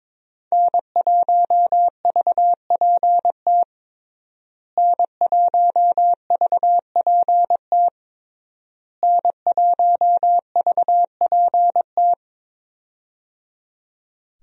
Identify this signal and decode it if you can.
Morse code, heard as a keyed tone, about 22 wpm, 690 Hz. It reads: N1VPT N1VPT N1VPT
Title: USA Call Signs - 5 Characters - Repeated 3x - 22wpm